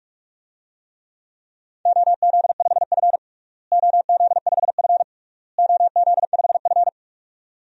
Morse code, heard as a keyed tone, about 46 wpm, 700 Hz. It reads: OZ5F OZ5F OZ5F
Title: DXCC-100 Call Signs - 4 Characters - Repeated 3x - 45wpm